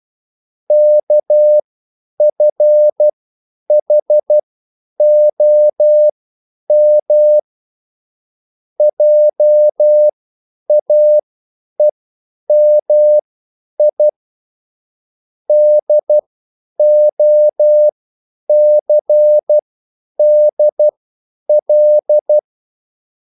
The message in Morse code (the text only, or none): KFHOM JAEMI DOCDL